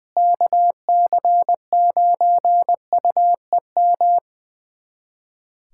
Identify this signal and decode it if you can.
Morse code, heard as a keyed tone, about 20 wpm, 700 Hz. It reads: KC9UEM
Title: USA Callsign DE Callsign - 6 Characters - 20wpm